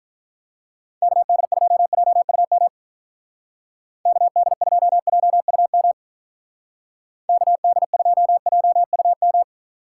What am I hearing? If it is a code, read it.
Morse code, heard as a keyed tone, about 45 wpm, 700 Hz. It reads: KD2JUM KD2JUM KD2JUM